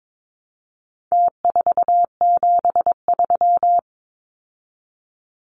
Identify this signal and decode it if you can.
Morse code, heard as a keyed tone, about 22 wpm, 700 Hz. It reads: T473
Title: Numbers - Contest Style - 4-Digit 22wpm